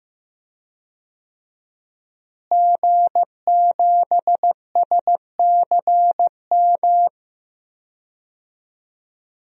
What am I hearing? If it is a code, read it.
Morse code, heard as a keyed tone, about 15 wpm, 700 Hz. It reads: G7SCM
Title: DXCC-100 Call Signs - 5 Characters - Repeated 2x - 15wpm